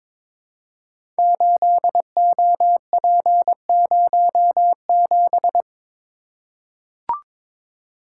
Morse code, heard as a keyed tone, about 22 wpm, 700 Hz. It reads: 8OP07
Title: Code Groups - 5 Letter:Numbers - 22wpm